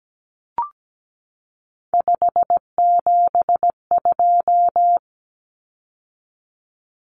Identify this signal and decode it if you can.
Morse code, heard as a keyed tone, about 17 wpm, 700 Hz. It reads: 572
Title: Common Cut Numbers - 3-Digit 17wpm